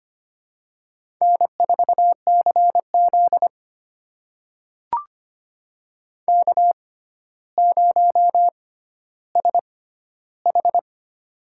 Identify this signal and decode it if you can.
Morse code, heard as a keyed tone, about 25 wpm, spaced as for 13 wpm, 700 Hz. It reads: N4CZ K0SH
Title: USA Call Signs - 4 Characters - 6x Character Spacing - 25wpm